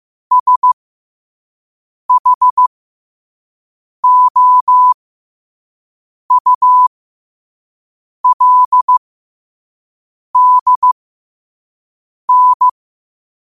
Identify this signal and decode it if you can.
Morse code, heard as a keyed tone, about 15 wpm, spaced as for 5 wpm, 1000 Hz. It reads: SHOULDN
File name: qotdfiles / 5wpm